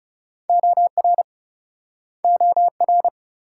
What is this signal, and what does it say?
Morse code, heard as a keyed tone, about 40 wpm, 700 Hz. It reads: OR OR